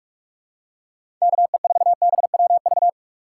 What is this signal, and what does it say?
Morse code, heard as a keyed tone, about 46 wpm, 700 Hz. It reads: KE4BWV
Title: USA Call Signs - 6 Characters - 6x Character Spacing - 45wpm